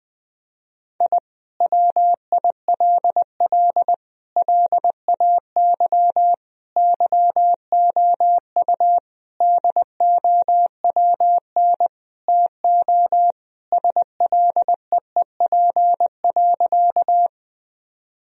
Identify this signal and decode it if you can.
Morse code, heard as a keyed tone, about 20 wpm, 700 Hz. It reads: I WILL LAY YOU DOWN TO SLEEP.
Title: Sentences from Top 1000 Words - Review Part 2 - 20wpm